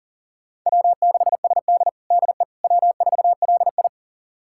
Morse code, heard as a keyed tone, about 40 wpm, 700 Hz. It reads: W6SD DE W4LI
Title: USA Callsign DE Callsign - 4 Characters - 40wpm